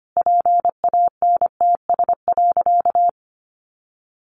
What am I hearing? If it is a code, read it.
Morse code, heard as a keyed tone, about 25 wpm, 700 Hz. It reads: PANTS.
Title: A Sound Of Thunder 25wpm